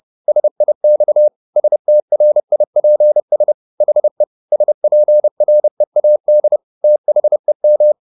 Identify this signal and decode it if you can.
Morse code, heard as a keyed tone, about 30 wpm, 600 Hz. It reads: SIX STRIPS HE SPREAD THEM
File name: the old man and the sea / the old man and the sea 1 30wpm